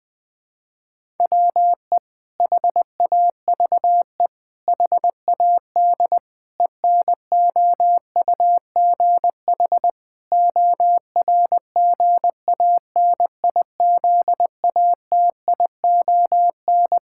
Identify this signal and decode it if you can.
Morse code, heard as a keyed tone, about 20 wpm, 700 Hz. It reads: WE HAVE HAD ENOUGH ORGANIZATION
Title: Sentences from Top 1000 Words - Review Part 5 - 20wpm